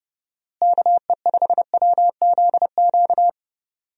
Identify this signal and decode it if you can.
Morse code, heard as a keyed tone, about 31 wpm, 700 Hz. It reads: KE5WZQ
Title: USA Call Signs - 6 Characters - Repeated 2x - 30wpm